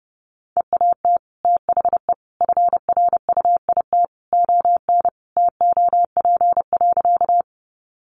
Morse code, heard as a keyed tone, about 29 wpm, 700 Hz. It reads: EAT THE FRUIT ON TOP.
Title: Sentences from Top 1300 Words 30wpm